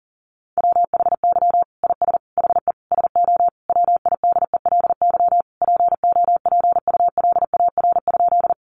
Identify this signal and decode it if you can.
Morse code, heard as a keyed tone, about 40 wpm, 700 Hz. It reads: WHY IS HE SO WIDELY POPULAR?
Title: Sentences from Top 1200 Words 40wpm